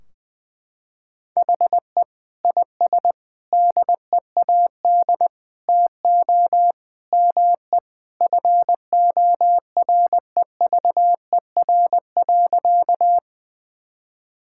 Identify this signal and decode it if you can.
Morse code, heard as a keyed tone, about 20 wpm, 700 Hz. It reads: HE IS DEAD TO ME FOREVER.
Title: Sentences from Top 1600 Words 20wpm